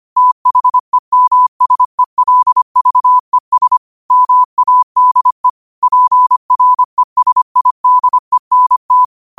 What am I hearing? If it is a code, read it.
Morse code, heard as a keyed tone, about 25 wpm, 1000 Hz. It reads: THEMSELVES MADE PRESIDENT
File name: qotdfiles / 25wpm